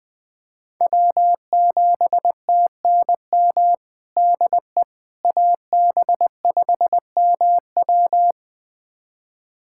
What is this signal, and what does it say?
Morse code, heard as a keyed tone, about 20 wpm, 700 Hz. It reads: W7TNM DE AB5MW
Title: USA Callsign DE Callsign - 5 Characters - 20wpm